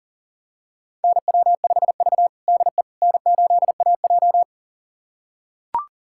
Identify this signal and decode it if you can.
Morse code, heard as a keyed tone, about 40 wpm, 700 Hz. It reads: NW5V DE N8AJ
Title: USA Callsign DE Callsign - 4 Characters - 40wpm